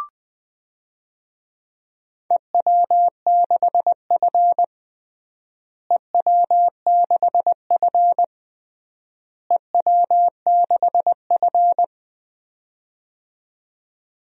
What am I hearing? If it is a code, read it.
Morse code, heard as a keyed tone, about 20 wpm, 690 Hz. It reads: EW6F EW6F EW6F
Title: DXCC-100 Call Signs - 4 Characters - Repeated 3x - 20wpm